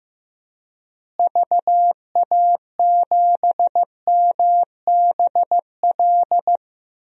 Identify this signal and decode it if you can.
Morse code, heard as a keyed tone, about 15 wpm, 700 Hz. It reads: VA7MBL
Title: DXCC-100 Call Signs - 6 Characters - Repeated 2x - 15wpm